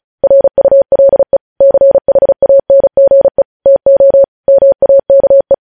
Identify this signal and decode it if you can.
Morse code, heard as a keyed tone, about 36 wpm, 550 Hz. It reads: RULE CHANGE TO MAKE
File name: podcast / News headlines-1x-35x35